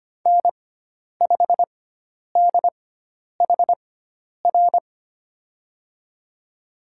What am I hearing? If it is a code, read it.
Morse code, heard as a keyed tone, about 25 wpm, spaced as for 10 wpm, 700 Hz. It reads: N5DHR
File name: Lesson 16g - Letter c - Call Signs 25wpm